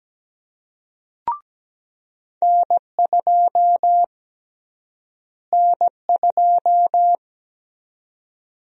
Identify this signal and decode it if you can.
Morse code, heard as a keyed tone, about 17 wpm, 700 Hz. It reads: N2 N2
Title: USA Call Sign Prefixes - Repeated 2x - 17wpm